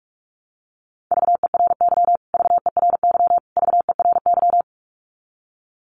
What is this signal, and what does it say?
Morse code, heard as a keyed tone, about 44 wpm, 700 Hz. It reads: VERY VERY VERY
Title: Top 500 Words Repeated 45wpm